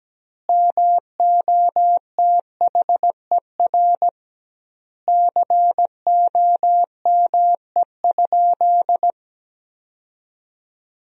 Morse code, heard as a keyed tone, about 17 wpm, 700 Hz. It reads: MOTHER COME?
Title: 2x Word-Spacing - Sentences from Top 200 Words - 17wpm